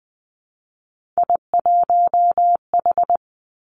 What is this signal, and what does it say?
Morse code, heard as a keyed tone, about 20 wpm, 700 Hz. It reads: I1H
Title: Three Challenging Letter-Number Combinations 20wpm